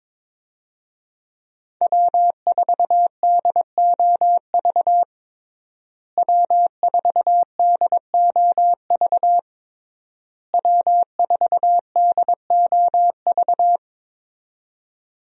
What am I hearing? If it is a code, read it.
Morse code, heard as a keyed tone, about 22 wpm, 700 Hz. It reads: W4DOV W4DOV W4DOV
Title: USA Call Signs - 5 Characters - Repeated 3x - 22wpm